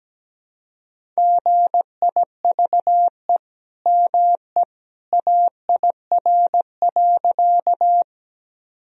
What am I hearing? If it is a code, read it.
Morse code, heard as a keyed tone, about 17 wpm, 700 Hz. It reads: GIVE ME AIR.
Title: Sentences from Top 1000 Words - Review Part 1 - 17wpm